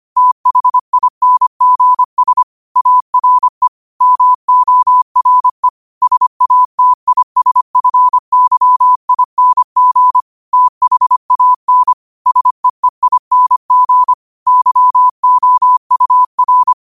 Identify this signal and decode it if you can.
Morse code, heard as a keyed tone, about 25 wpm, 1000 Hz. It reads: THINGS ARE MORE SATISFYING THAN SEEING YOUR